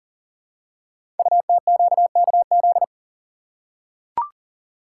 Morse code, heard as a keyed tone, about 42 wpm, 690 Hz. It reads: UTQKZ
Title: Code Groups - 5 Letters - 40wpm